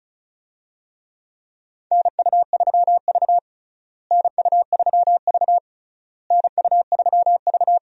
Morse code, heard as a keyed tone, about 36 wpm, 700 Hz. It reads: NU3V NU3V NU3V
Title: USA Call Signs - 4 Characters - Repeated 3x - 35wpm